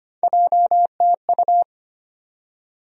Morse code, heard as a keyed tone, about 25 wpm, 700 Hz. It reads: JTU
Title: USA Call Sign Postfixes - 3 Letters 25wpm